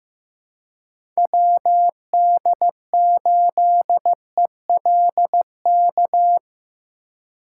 Morse code, heard as a keyed tone, about 17 wpm, 700 Hz. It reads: WD8ELK
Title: Speed Racing - USA 2x3 Call Signs 15wpm